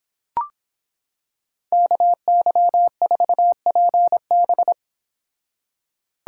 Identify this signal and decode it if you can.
Morse code, heard as a keyed tone, about 26 wpm, 700 Hz. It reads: KY4PB